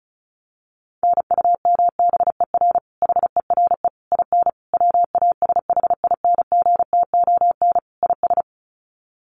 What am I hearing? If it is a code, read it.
Morse code, heard as a keyed tone, about 34 wpm, 700 Hz. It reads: NUMBER HERE IN WASHINGTON IS
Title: Sets of 5 Words Encore2 35wpm